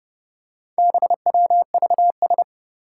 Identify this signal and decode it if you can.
Morse code, heard as a keyed tone, about 30 wpm, 700 Hz. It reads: BWVS
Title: Four Challenging Letter-Number Combinations 30wpm